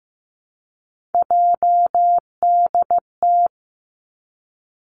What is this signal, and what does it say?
Morse code, heard as a keyed tone, about 15 wpm, 700 Hz. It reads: JDT
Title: Three Challenging Letter-Number Combinations 15wpm